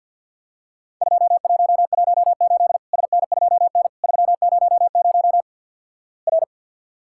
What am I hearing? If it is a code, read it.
Morse code, heard as a keyed tone, about 50 wpm, 700 Hz. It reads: 2118 SN2N 300 R
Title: Worked All Europe DX Contest - QTC Exchange with Repeats - 50wpm